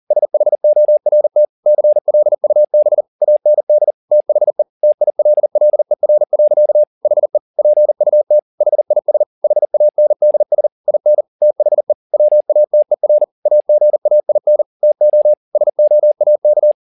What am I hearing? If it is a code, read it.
Morse code, heard as a keyed tone, about 40 wpm, 600 Hz. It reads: SHORT CLUB AND THE TILLER. HE PUT HIS HANDS IN THE WATER AGAIN TO SOAK
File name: the old man and the sea 2 40wpm